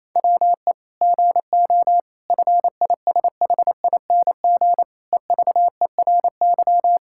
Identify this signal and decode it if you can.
Morse code, heard as a keyed tone, about 28 wpm, 700 Hz. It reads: WE GO FISHING EVERY
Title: Sentences from Top 1000 Words - Review Part 5 - 28wpm